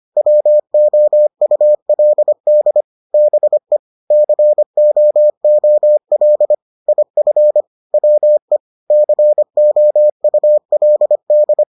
Morse code, heard as a keyed tone, about 25 wpm, 600 Hz. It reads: WOULD BE COOL IF WE COULD